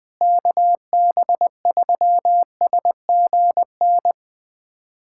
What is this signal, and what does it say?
Morse code, heard as a keyed tone, about 20 wpm, 700 Hz. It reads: KB3SGN
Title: USA Callsign DE Callsign 20wpm